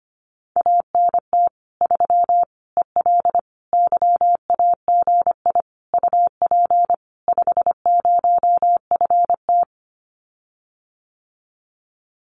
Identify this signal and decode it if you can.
Morse code, heard as a keyed tone, about 25 wpm, 700 Hz. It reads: ANT 3 EL YAGI UP 50FT
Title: Common QSO Phrases Encore 25wpm